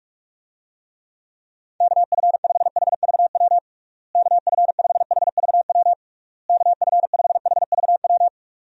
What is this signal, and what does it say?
Morse code, heard as a keyed tone, about 43 wpm, 700 Hz. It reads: KF5HVW KF5HVW KF5HVW
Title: USA Call Signs - 6 Characters - Repeated 3x - 45wpm